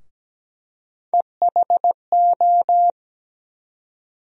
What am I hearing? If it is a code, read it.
Morse code, heard as a keyed tone, about 17 wpm, 700 Hz. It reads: EHO
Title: USA Call Sign Postfixes - 3 Letters 17wpm